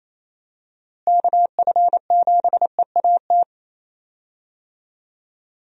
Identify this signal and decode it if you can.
Morse code, heard as a keyed tone, about 28 wpm, 700 Hz. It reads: KF7EAT
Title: USA Call Signs - 6 Characters - Repeated 3x - 28wpm